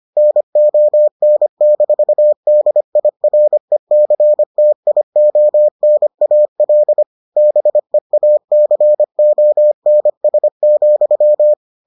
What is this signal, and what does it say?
Morse code, heard as a keyed tone, about 25 wpm, 600 Hz. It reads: NON-DIRECTIONAL BEACONS,